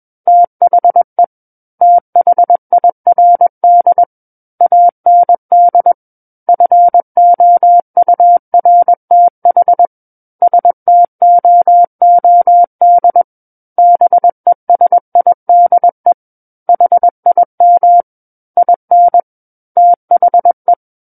Morse code, heard as a keyed tone, about 21 wpm, 700 Hz. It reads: THE THIRD AND FOURTH STOOD BESIDE HIM IN THE
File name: Book One / 12-Chapter 12 - 21 WPM